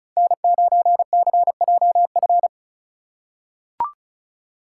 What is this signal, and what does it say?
Morse code, heard as a keyed tone, about 34 wpm, 700 Hz. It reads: N9CJF
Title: USA Call Signs - 5 Characters - 2x Character Spacing - 35wpm